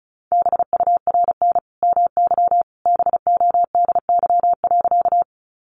Morse code, heard as a keyed tone, about 34 wpm, 700 Hz. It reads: BURN MY BODY.
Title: Sentences from Top 1300 Words 35wpm